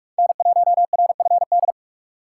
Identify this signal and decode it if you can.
Morse code, heard as a keyed tone, about 44 wpm, 700 Hz. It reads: N1RFD